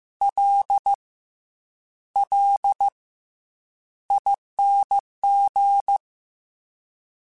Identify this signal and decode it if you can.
Morse code, heard as a keyed tone, about 15 wpm, 800 Hz. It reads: L L ING
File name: Special INGs / Special INGs-mp3s / SlowToFaster ing @13